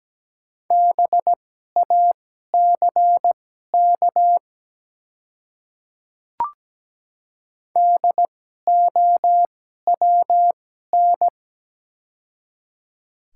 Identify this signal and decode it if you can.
Morse code, heard as a keyed tone, about 17 wpm, spaced as for 12 wpm, 700 Hz. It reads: BACK DOWN
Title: 2x Character Spacing - Top 200 Words - 17wpm